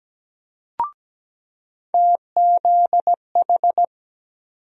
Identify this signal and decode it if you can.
Morse code, heard as a keyed tone, about 17 wpm, 700 Hz. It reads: TZH
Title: USA Call Sign Postfixes - 3 Letters 17wpm